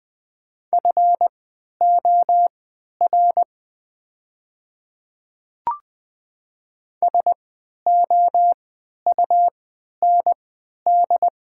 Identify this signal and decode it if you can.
Morse code, heard as a keyed tone, about 20 wpm, spaced as for 11 wpm, 700 Hz. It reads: FOR SOUND
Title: 3x Character Spacing - Top 300 Words - 20wpm